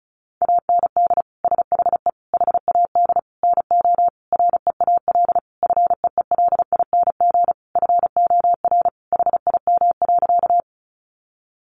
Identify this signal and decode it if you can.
Morse code, heard as a keyed tone, about 34 wpm, 700 Hz. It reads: AND SHE HAD NO REAL FEELING FOR HIM.
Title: Sentences from Top 1200 Words 35wpm